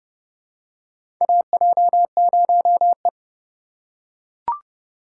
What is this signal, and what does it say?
Morse code, heard as a keyed tone, about 30 wpm, 700 Hz. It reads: AJ0E
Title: USA Call Signs - 2x Character Spacing - 30wpm